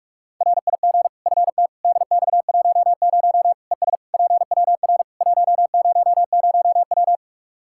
Morse code, heard as a keyed tone, about 42 wpm, 700 Hz. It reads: RIG FT DX10 ES PWR 100W
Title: Practice QSO - Exchange 2 - Info - 45wpm